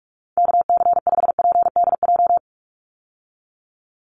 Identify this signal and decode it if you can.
Morse code, heard as a keyed tone, about 44 wpm, 700 Hz. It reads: KC5PDJ
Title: USA Call Signs 45wpm